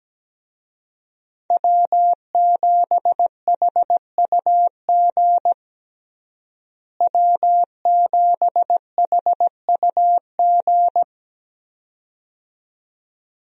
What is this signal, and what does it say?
Morse code, heard as a keyed tone, about 17 wpm, 700 Hz. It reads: W7HUG W7HUG
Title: USA Call Signs - 5 Characters - Repeated 2x - 17wpm